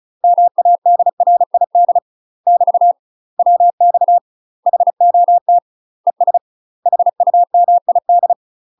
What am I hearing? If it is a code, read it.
Morse code, heard as a keyed tone, about 34 wpm, 700 Hz. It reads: MADRID <BT> WX HOT ES HUMID
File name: UK MASTER QSOs TRAINING / 35WPM / UK MASTER QSOs 0039